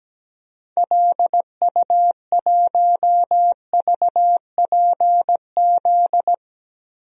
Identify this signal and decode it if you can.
Morse code, heard as a keyed tone, about 17 wpm, 700 Hz. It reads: LU1VPZ
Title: DXCC-100 Call Signs - 6 Characters - Repeated 2x - 17wpm